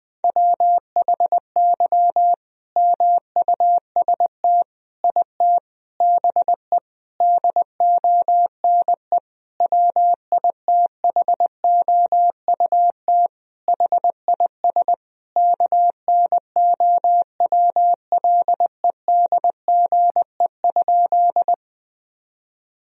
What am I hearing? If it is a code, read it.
Morse code, heard as a keyed tone, about 21 wpm, 700 Hz. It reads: WHY MUST IT BE DONE WITHOUT HIS KNOWLEDGE?
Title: Sentences from Top 1000 Words - Review Part 3 - 20wpm